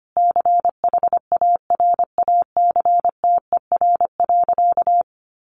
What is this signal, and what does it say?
Morse code, heard as a keyed tone, about 25 wpm, 700 Hz. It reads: CHARACTER.
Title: Sentences from Top 1200 Words 25wpm